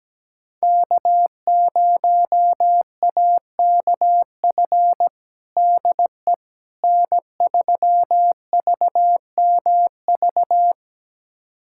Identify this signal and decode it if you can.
Morse code, heard as a keyed tone, about 17 wpm, 700 Hz. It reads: K0AKF DE N3VMV
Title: USA Callsign DE Callsign - 5 Characters - 17wpm